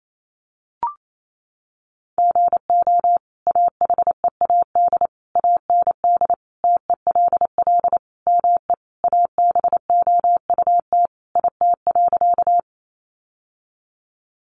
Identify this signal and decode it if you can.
Morse code, heard as a keyed tone, about 28 wpm, 700 Hz. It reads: GO AHEAD AND TELL ME ABOUT IT.
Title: Sentences from Top 1100 Words 28wpm